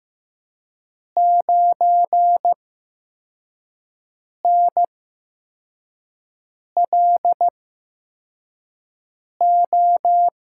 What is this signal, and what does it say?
Morse code, heard as a keyed tone, about 15 wpm, spaced as for 4 wpm, 700 Hz. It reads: 9NLO
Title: USA Call Signs - 6 Characters - 8x Character Spacing - 15wpm